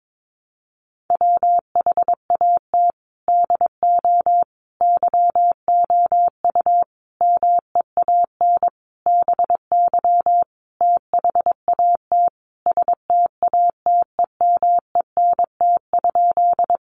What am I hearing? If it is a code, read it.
Morse code, heard as a keyed tone, about 22 wpm, 700 Hz. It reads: WHAT DO YOU MEAN BY THAT STATEMENT?
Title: Sentences from Top 1200 Words 22wpm